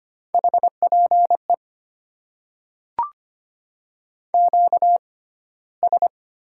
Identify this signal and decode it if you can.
Morse code, heard as a keyed tone, about 25 wpm, 700 Hz. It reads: H P E Q S
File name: Top 90 QSO Elements - 6x Character Spacing - 25wpm